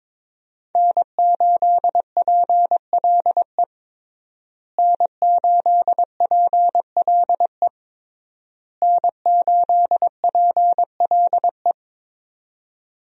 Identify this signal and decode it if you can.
Morse code, heard as a keyed tone, about 22 wpm, 700 Hz. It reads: N8PLE N8PLE N8PLE
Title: USA Call Signs - 5 Characters - Repeated 3x - 22wpm